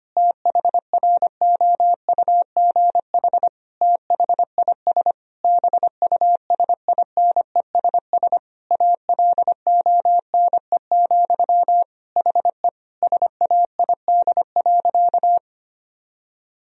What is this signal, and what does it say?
Morse code, heard as a keyed tone, about 25 wpm, 700 Hz. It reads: THROUGH THIS BUSINESS ALONE, HE SAID.
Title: Sentences from Top 1000 Words - Review Part 2 - 25wpm